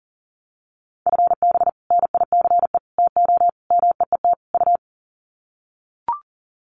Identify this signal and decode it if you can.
Morse code, heard as a keyed tone, about 38 wpm, 700 Hz. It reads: FB NICE TO MEET U
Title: Common QSO Phrases 40wpm